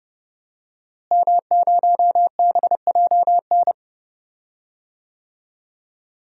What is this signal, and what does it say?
Morse code, heard as a keyed tone, about 30 wpm, 700 Hz. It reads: M0BJN
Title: DXCC-100 Call Signs - 5 Characters - Repeated 3x - 30wpm